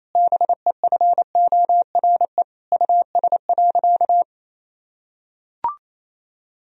Morse code, heard as a keyed tone, about 28 wpm, 700 Hz. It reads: BEFORE US.
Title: Sentences from Top 1000 Words - Review Part 4 - 28wpm